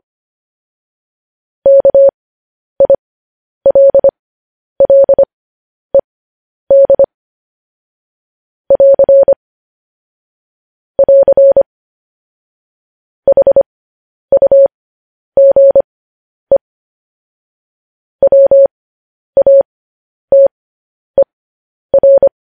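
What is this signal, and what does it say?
Morse code, heard as a keyed tone, about 25 wpm, spaced as for 10 wpm, 550 Hz. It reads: KILLED <AR> <AR> HUGE WATER
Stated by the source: Morse code news headlines sent at 10 wpm.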